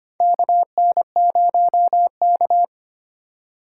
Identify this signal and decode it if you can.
Morse code, heard as a keyed tone, about 25 wpm, 700 Hz. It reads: KN0K
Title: USA Call Signs - 4x Character Spacing - 25wpm